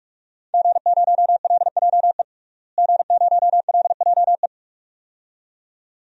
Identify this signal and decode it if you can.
Morse code, heard as a keyed tone, about 46 wpm, 700 Hz. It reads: G0LJE G0LJE
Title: DXCC-100 Call Signs - 5 Characters - Repeated 2x - 45wpm